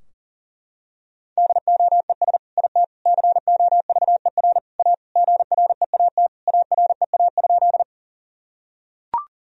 Morse code, heard as a keyed tone, about 40 wpm, 700 Hz. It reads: DOES IT COVER A GREAT AREA?
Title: Sentences from Top 1000 Words - Review Part 2 - 40wpm